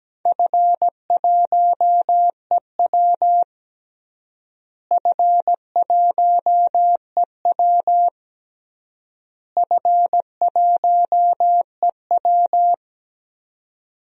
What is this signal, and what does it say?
Morse code, heard as a keyed tone, about 17 wpm, 700 Hz. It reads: F1EW F1EW F1EW
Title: DXCC-100 Call Signs - 4 Characters - Repeated 3x - 17wpm